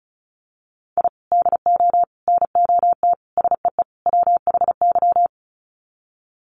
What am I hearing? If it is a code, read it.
Morse code, heard as a keyed tone, about 34 wpm, 700 Hz. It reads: I DO NOT SEE WHY
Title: Sets of 5 Words Encore 35wpm